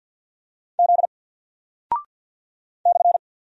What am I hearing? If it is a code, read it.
Morse code, heard as a keyed tone, about 48 wpm, 700 Hz. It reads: Z /
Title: Single Character (Letter:Number:Punctuation:Prosign) Alt 50wpm